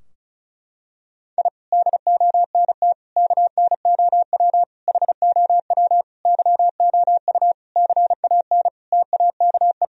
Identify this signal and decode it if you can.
Morse code, heard as a keyed tone, about 36 wpm, 700 Hz. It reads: I DONT KNOW HOW YOU CAN TAKE
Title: Sentences from Top 1000 Words - Review Part 2 - 35wpm